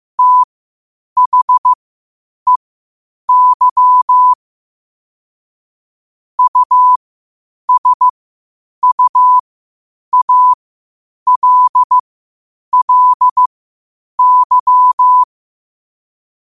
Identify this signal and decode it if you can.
Morse code, heard as a keyed tone, about 15 wpm, spaced as for 8 wpm, 1000 Hz. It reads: THEY USUALLY
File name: qotdfiles / 8wpm